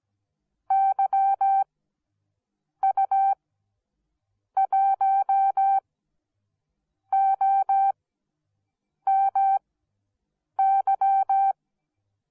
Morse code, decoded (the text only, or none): YU1OMY